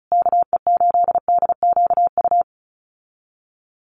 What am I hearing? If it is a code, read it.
Morse code, heard as a keyed tone, about 34 wpm, 700 Hz. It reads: KE8DQU
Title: USA Call Signs Repeated 3x 35wpm